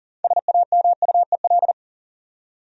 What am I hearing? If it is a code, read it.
Morse code, heard as a keyed tone, about 40 wpm, 700 Hz. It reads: SAMUEL
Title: USA Common Names - 40wpm